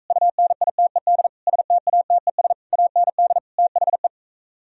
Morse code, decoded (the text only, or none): UNITED STATES AND THE